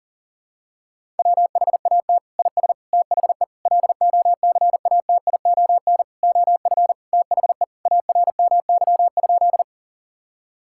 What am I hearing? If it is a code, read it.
Morse code, heard as a keyed tone, about 37 wpm, 700 Hz. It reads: WHAT IS THE LOCATION OF THE ARMY?
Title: Sentences from Top 1000 Words - Review Part 6 - 40wpm